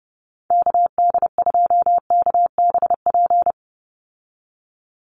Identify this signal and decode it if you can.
Morse code, heard as a keyed tone, about 30 wpm, 700 Hz. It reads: KD2KBP
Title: USA Call Signs 30wpm